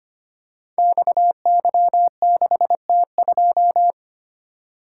Morse code, heard as a keyed tone, about 25 wpm, 700 Hz. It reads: XY6T2
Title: Code Groups - 5 Letter:Numbers - 25wpm